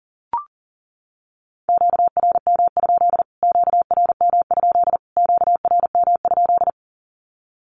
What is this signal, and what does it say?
Morse code, heard as a keyed tone, about 40 wpm, 700 Hz. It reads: QRM? QRM? QRM?
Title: Q-Codes Repeated 3x 40wpm